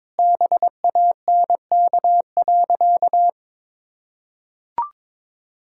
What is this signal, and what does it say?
Morse code, heard as a keyed tone, about 22 wpm, 700 Hz. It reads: BANK.
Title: Sentences from Top 1000 Words - Review Part 6 - 22wpm